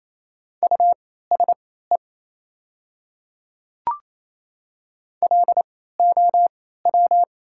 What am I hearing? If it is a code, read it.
Morse code, heard as a keyed tone, about 28 wpm, spaced as for 16 wpm, 700 Hz. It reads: USE LOW
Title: 3x Character Spacing - Top 300 Words - 28wpm